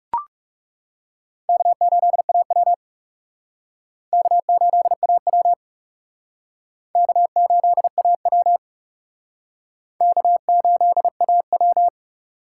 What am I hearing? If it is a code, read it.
Morse code, heard as a keyed tone, about 40 wpm, 700 Hz. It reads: K8AW K8AW K8AW K8AW